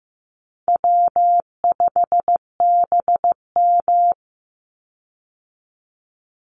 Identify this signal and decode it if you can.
Morse code, heard as a keyed tone, about 15 wpm, 700 Hz. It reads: W5BM
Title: Easy Call Signs - 15wpm